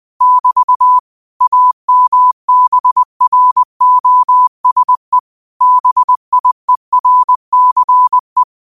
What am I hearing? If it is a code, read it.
Morse code, heard as a keyed tone, about 20 wpm, 1000 Hz. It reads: <BT> AMBROSE BIERCE
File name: 20wpm